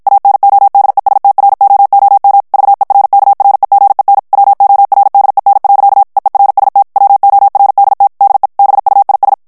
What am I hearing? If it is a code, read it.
Morse code, heard as a keyed tone, about 56 wpm, 800 Hz. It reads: ANODESTROOM VERKREGEN WORDEN. EERST WORDT DE BUIS
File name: HOLLANDAIS texte clair 2 vitesse 50 mots minute